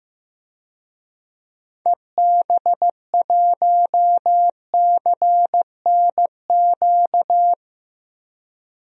Morse code, heard as a keyed tone, about 15 wpm, 700 Hz. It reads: EB1CNQ
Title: DXCC-100 Call Signs - 6 Characters - 15wpm